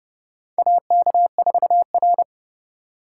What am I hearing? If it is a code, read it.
Morse code, heard as a keyed tone, about 30 wpm, 700 Hz. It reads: AK4R
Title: USA Call Signs - 8x Character Spacing - 30wpm